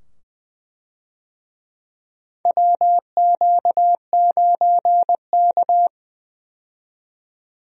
Morse code, heard as a keyed tone, about 20 wpm, 700 Hz. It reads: WQ9K
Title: USA Call Signs - 4 Characters - 20wpm